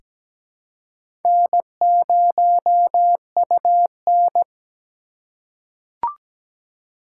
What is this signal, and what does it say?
Morse code, heard as a keyed tone, about 17 wpm, 700 Hz. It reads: N0UN
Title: USA Call Signs - 4 Characters - 2x Character Spacing - 17wpm